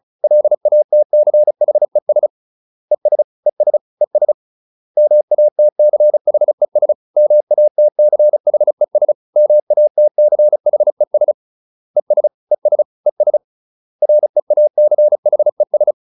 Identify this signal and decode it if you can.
Morse code, heard as a keyed tone, about 34 wpm, 600 Hz. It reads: LATCHES ES ES ES MATCHES MATCHES MATCHES ES ES ES REACHES